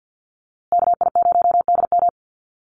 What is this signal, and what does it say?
Morse code, heard as a keyed tone, about 48 wpm, 700 Hz. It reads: KI0DM
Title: USA Call Signs Repeated 3x 50wpm